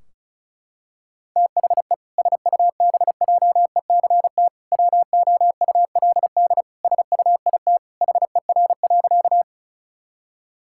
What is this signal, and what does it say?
Morse code, heard as a keyed tone, about 36 wpm, 700 Hz. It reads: THE SUBJECT WOULD SUIT HER.
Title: Sentences from Top 1600 Words 35wpm